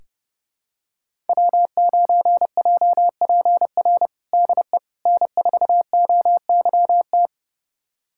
Morse code, heard as a keyed tone, about 30 wpm, 700 Hz. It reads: W9JPR DE N4OYT
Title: USA Callsign DE Callsign - 5 Characters - 30wpm